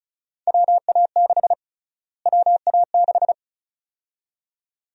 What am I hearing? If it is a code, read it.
Morse code, heard as a keyed tone, about 36 wpm, 700 Hz. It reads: WA6 WA6
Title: USA Call Sign Prefixes - Repeated 2x - 35wpm